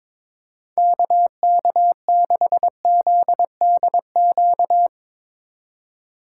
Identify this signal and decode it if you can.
Morse code, heard as a keyed tone, about 22 wpm, 700 Hz. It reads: KK6ZDQ
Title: USA Call Signs - 6 Characters - 2x Character Spacing - 22wpm